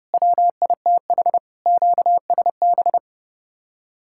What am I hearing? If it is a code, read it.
Morse code, heard as a keyed tone, about 37 wpm, 700 Hz. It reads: WITH QSB
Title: Speed-Racing Common QSO Phrases - 30wpm